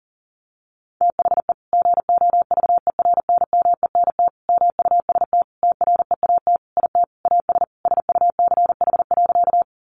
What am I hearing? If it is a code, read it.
Morse code, heard as a keyed tone, about 40 wpm, 700 Hz. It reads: THE GOVERNMENT MUST TREAT IT AS SUCH.
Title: Sentences from Top 1300 Words 40wpm